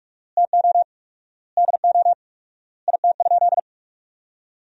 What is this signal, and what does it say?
Morse code, heard as a keyed tone, about 42 wpm, 700 Hz. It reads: TO DO IT?
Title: 4x Word-Spacing - Sentences from Top 200 Words - 45wpm